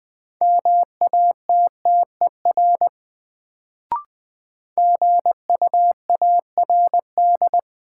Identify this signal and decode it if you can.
Morse code, heard as a keyed tone, about 20 wpm, 700 Hz. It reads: MATTER GUARD
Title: Sentences from Top 700 Words - New Word Drill - 20wpm